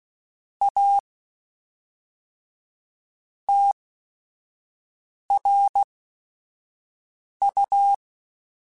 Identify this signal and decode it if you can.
Morse code, heard as a keyed tone, about 16 wpm, spaced as for 5 wpm, 800 Hz. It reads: A TRU
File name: Technical Subjects / Technical Subjects-mp3s / AND Gate @5